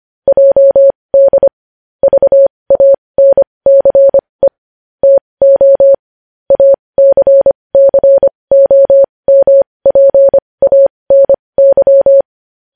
Morse code, heard as a keyed tone, about 25 wpm, spaced as for 20 wpm, 550 Hz. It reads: JD VANCE TO ACCOMPANY
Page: news-headlines-in-morse-code-at-20-wpm